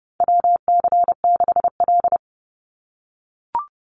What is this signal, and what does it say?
Morse code, heard as a keyed tone, about 30 wpm, 700 Hz. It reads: WC6L
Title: USA Call Signs 30wpm